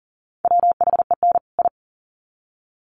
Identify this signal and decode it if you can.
Morse code, heard as a keyed tone, about 40 wpm, 700 Hz. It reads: WHEN I
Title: Sets of 2 Words 40wpm